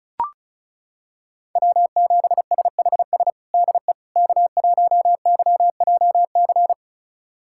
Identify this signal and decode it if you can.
Morse code, heard as a keyed tone, about 36 wpm, 700 Hz. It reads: W7SHS DE K1YJC
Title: USA Callsign DE Callsign - 5 Characters - 35wpm